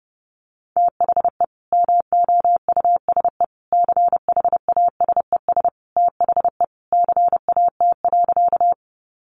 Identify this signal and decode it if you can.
Morse code, heard as a keyed tone, about 30 wpm, 700 Hz. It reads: THE MOUSE CHASES THE CAT.
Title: Easy Phrases 30wpm